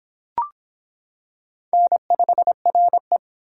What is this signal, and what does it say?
Morse code, heard as a keyed tone, about 25 wpm, 700 Hz. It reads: N5RE